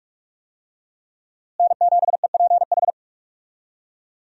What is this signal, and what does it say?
Morse code, heard as a keyed tone, about 46 wpm, 700 Hz. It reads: N7EPH
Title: USA Call Signs - 5 Characters - Repeated 2x - 45wpm